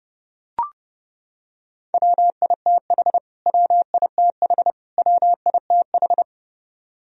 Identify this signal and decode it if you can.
Morse code, heard as a keyed tone, about 31 wpm, 700 Hz. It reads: WITH WITH WITH
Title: Top 100 Words Repeated 30wpm